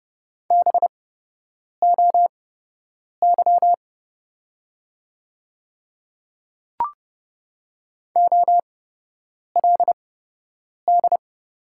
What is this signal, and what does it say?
Morse code, heard as a keyed tone, about 30 wpm, spaced as for 8 wpm, 700 Hz. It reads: BOY OLD
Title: 8x Character Spacing - Top 200 Words - 30wpm